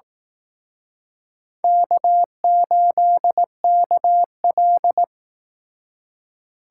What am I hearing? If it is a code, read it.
Morse code, heard as a keyed tone, about 18 wpm, 700 Hz. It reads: K8KL
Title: Speed Racing - USA 1x2 Call Signs 15wpm